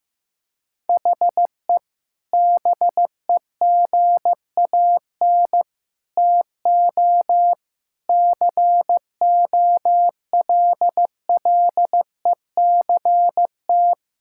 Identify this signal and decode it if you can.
Morse code, heard as a keyed tone, about 15 wpm, 700 Hz. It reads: HE BEGAN TO COLLECT
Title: Sentences from Top 1600 Words 15wpm